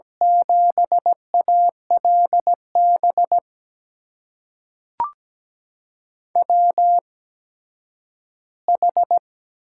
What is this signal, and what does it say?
Morse code, heard as a keyed tone, about 17 wpm, spaced as for 10 wpm, 700 Hz. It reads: 7ALB WH
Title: USA Call Signs - 6 Characters - 8x Character Spacing - 17wpm